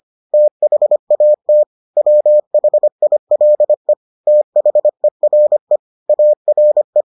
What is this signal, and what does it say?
Morse code, heard as a keyed tone, about 25 wpm, 600 Hz. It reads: THAT WHILE THERE ARE